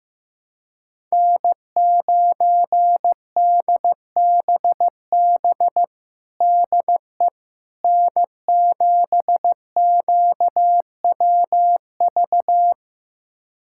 Morse code, heard as a keyed tone, about 15 wpm, 700 Hz. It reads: N9DBB DE N7QWV
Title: USA Callsign DE Callsign - 5 Characters - 15wpm